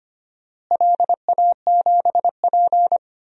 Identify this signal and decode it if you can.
Morse code, heard as a keyed tone, about 25 wpm, 700 Hz. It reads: LA7P
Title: DXCC-100 Call Signs - 4 Characters - Repeated 2x - 25wpm